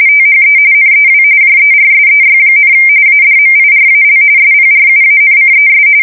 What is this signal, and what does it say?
RTTY, about 45.45 baud. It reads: THE NEW JERSEY, ADMIRAL SPRUANCES
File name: fsk170 / USS-New-Jersey